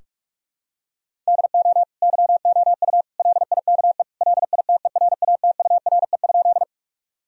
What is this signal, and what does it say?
Morse code, heard as a keyed tone, about 46 wpm, 700 Hz. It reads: DO YOU LIKE LITERATURE?
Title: Sentences from Top 1000 Words - Review Part 3 - 45wpm